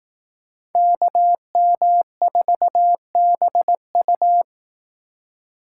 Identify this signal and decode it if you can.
Morse code, heard as a keyed tone, about 18 wpm, 700 Hz. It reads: KM4BU
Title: Speed Racing - USA 2x2 Call Signs 15wpm